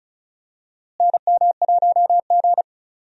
Morse code, heard as a keyed tone, about 36 wpm, 700 Hz. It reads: NM1G
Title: USA Call Signs - 4 Characters - Repeated 2x - 35wpm